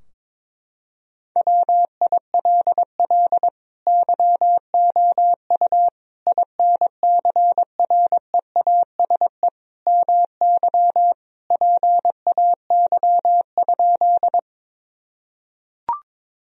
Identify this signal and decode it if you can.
Morse code, heard as a keyed tone, about 22 wpm, 700 Hz. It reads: WILL YOU INCREASE MY PAY?
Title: Sentences from Top 1000 Words - Review Part 4 - 22wpm